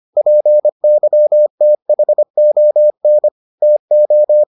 Morse code, heard as a keyed tone, about 25 wpm, 600 Hz. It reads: PYTHON TO